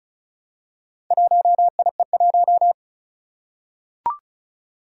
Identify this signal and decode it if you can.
Morse code, heard as a keyed tone, about 36 wpm, 700 Hz. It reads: 1IE1
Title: Four Challenging Letter-Number Combinations 35wpm